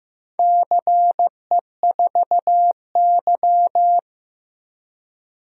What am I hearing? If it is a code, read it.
Morse code, heard as a keyed tone, about 15 wpm, 700 Hz. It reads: CE4Y
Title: DXCC-100 Call Signs - 4 Characters - 15wpm